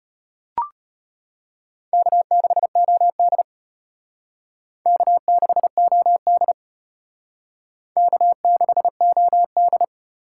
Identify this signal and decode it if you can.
Morse code, heard as a keyed tone, about 34 wpm, 700 Hz. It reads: K6OD K6OD K6OD